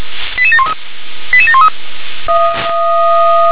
Five-tone selective call (ZVEI), heard as quick runs of five tones, tones of about 70 ms. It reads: 88713 88713